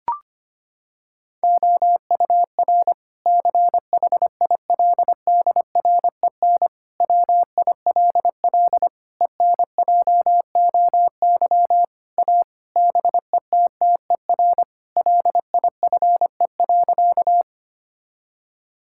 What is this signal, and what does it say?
Morse code, heard as a keyed tone, about 25 wpm, 700 Hz. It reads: OUR CHILDREN WILL ENJOY A BETTER LIFE.
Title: Sentences from Top 1000 Words - Review Part 5 - 25wpm